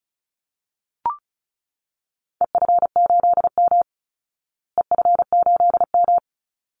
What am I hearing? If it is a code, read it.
Morse code, heard as a keyed tone, about 34 wpm, 700 Hz. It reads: EF8M EF8M
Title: Call Signs Repeated 3x - 35wpm